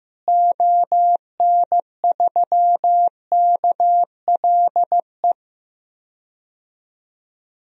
Morse code, heard as a keyed tone, about 15 wpm, 700 Hz. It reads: ON3KLE
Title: DXCC-100 Call Signs - 6 Characters - Repeated 2x - 15wpm